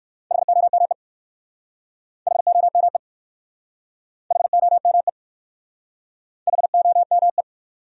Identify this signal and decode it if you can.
Morse code, heard as a keyed tone, about 58 wpm, 700 Hz. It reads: HOME HOME HOME HOME